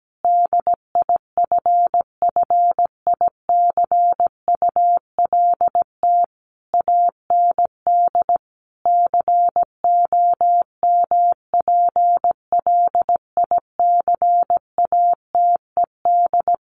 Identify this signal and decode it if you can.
Morse code, heard as a keyed tone, about 17 wpm, 700 Hz. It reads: DIFFICULT AND COMPLICATED
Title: Sentences from Top 1200 Words 17wpm